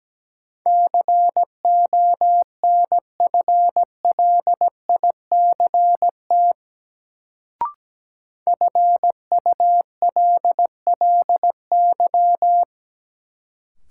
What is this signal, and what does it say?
Morse code, heard as a keyed tone, about 17 wpm, 700 Hz. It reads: CONFLICT FULLY
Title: Sentences from Top 1300 Words - New Word Drill Alt - 17wpm